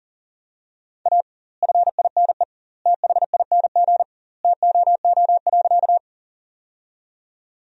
Morse code, heard as a keyed tone, about 40 wpm, 700 Hz. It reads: A FINE THING TOO.
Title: 2x Word-Spacing - Sentences from Top 500 Words - 40wpm